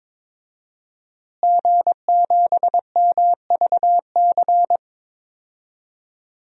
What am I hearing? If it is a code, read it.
Morse code, heard as a keyed tone, about 22 wpm, 700 Hz. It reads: G7MVC
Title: DXCC-100 Call Signs - 5 Characters - Repeated 3x - 22wpm